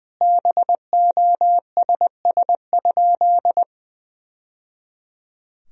Morse code, heard as a keyed tone, about 20 wpm, 700 Hz. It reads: BOSS?
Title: Sentences from Top 1000 Words - Review Part 4 - 20wpm